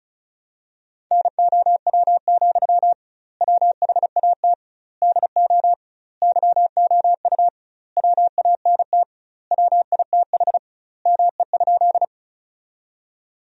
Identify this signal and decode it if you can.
Morse code, heard as a keyed tone, about 36 wpm, 700 Hz. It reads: NOW, WHAT DO YOU WANT WITH ME?
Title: 2x Word-Spacing - Sentences from Top 200 Words - 35wpm